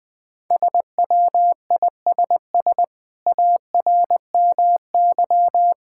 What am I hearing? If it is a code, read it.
Morse code, heard as a keyed tone, about 20 wpm, 700 Hz. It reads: SWISS ARMY
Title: Difficult Words - Speed-Racing with Hints - 20wpm